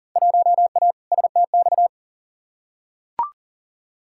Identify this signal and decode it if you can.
Morse code, heard as a keyed tone, about 40 wpm, 700 Hz. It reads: 1A STX
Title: Field Day Exchanges - 40wpm